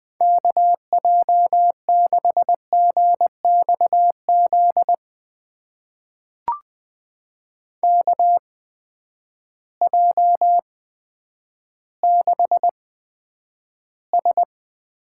Morse code, heard as a keyed tone, about 21 wpm, spaced as for 10 wpm, 700 Hz. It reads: KJ6GXZ KJ6S
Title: USA Call Signs - 6 Characters - 8x Character Spacing - 20wpm